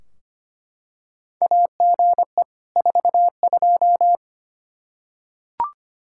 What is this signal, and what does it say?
Morse code, heard as a keyed tone, about 25 wpm, 700 Hz. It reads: AGE 42
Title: Practice QSO - Exchange 3 - More Info - 25wpm